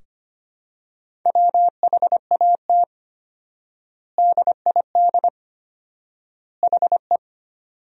Morse code, heard as a keyed tone, about 25 wpm, 700 Hz. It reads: WHAT DID HE
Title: 4x Word-Spacing - Sentences from Top 100 Words - 25wpm